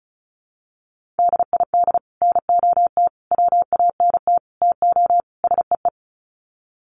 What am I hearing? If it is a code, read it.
Morse code, heard as a keyed tone, about 34 wpm, 700 Hz. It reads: DID NOT WANT TO SEE
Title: Sets of 5 Words Encore2 35wpm